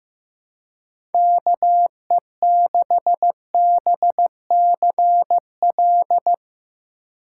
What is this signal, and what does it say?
Morse code, heard as a keyed tone, about 15 wpm, 700 Hz. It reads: KE6BCL
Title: USA Call Signs - 6 Characters - 3x Character Spacing - 15wpm